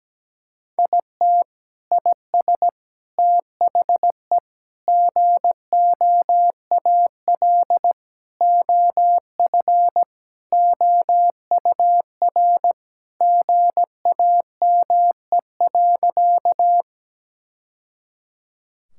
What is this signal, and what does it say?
Morse code, heard as a keyed tone, about 17 wpm, 700 Hz. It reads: IT IS THE GOAL OF OUR GAME.
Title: Sentences from Top 1000 Words - Review Part 5 - 17wpm